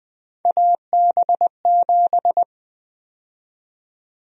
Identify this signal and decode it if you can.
Morse code, heard as a keyed tone, about 20 wpm, 700 Hz. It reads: AB7
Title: USA Call Sign Prefixes - Repeated 2x - 20wpm